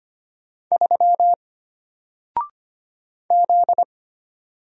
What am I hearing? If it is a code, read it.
Morse code, heard as a keyed tone, about 25 wpm, 700 Hz. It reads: 3 Z
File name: Single-Letter-Number-25wpm